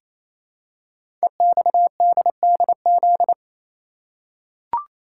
Morse code, heard as a keyed tone, about 27 wpm, 700 Hz. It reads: EXDDZ
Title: Code Groups - 5 Letters - 28wpm